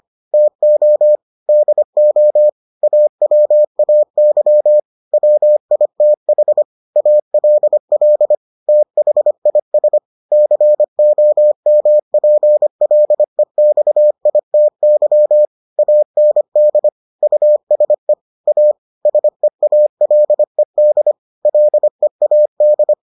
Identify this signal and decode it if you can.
Morse code, heard as a keyed tone, about 25 wpm, 600 Hz. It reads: TO DO AWAY WITH ALL THIS COMPLEXITY AND USE A SEALED LEAD